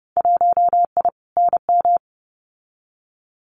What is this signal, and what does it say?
Morse code, heard as a keyed tone, about 30 wpm, 700 Hz. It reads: 1I NM
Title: Winter Field Day Exchanges Encore 30wpm